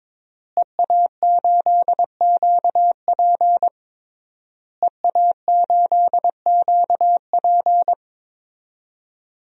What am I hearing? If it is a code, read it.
Morse code, heard as a keyed tone, about 22 wpm, 700 Hz. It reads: EA8QP EA8QP
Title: DXCC-100 Call Signs - 5 Characters - Repeated 3x - 22wpm